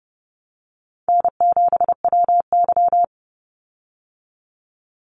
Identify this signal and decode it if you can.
Morse code, heard as a keyed tone, about 30 wpm, 700 Hz. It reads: N7WY
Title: Easy Call Signs Repeated 3x - 30wpm